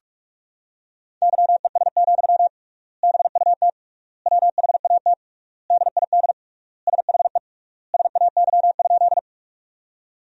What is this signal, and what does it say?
Morse code, heard as a keyed tone, about 41 wpm, 700 Hz. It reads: YES, BUT WHAT DID SHE SAY?
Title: 3x Word-Spacing - Sentences from Top 500 Words - 45wpm